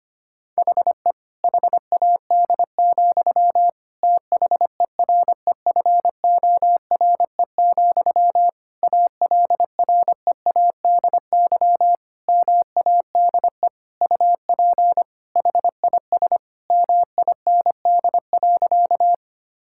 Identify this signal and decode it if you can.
Morse code, heard as a keyed tone, about 25 wpm, 700 Hz. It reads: HE HAD, THEREFORE, ALREADY MADE UP HIS MIND.
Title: Sentences from Top 1000 Words - Review Part 5 - 25wpm